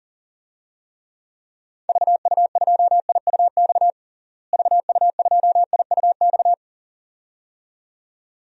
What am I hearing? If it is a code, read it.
Morse code, heard as a keyed tone, about 40 wpm, 700 Hz. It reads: VU2IUX VU2IUX
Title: DXCC-100 Call Signs - 6 Characters - Repeated 2x - 40wpm